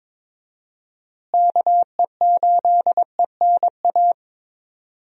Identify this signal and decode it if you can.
Morse code, heard as a keyed tone, about 21 wpm, 700 Hz. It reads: KE8ENA
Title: USA Call Signs - 6 Characters - Repeated 3x - 22wpm